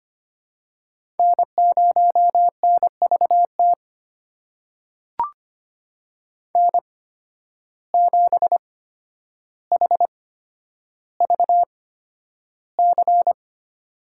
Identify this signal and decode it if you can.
Morse code, heard as a keyed tone, about 25 wpm, spaced as for 11 wpm, 700 Hz. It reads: N0NVT N7HVC
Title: USA Call Signs - 5 Characters - 8x Character Spacing - 25wpm